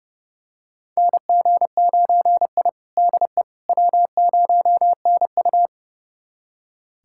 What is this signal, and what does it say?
Morse code, heard as a keyed tone, about 30 wpm, 700 Hz. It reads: NG9I DE W0NU
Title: USA Callsign DE Callsign - 4 Characters - 30wpm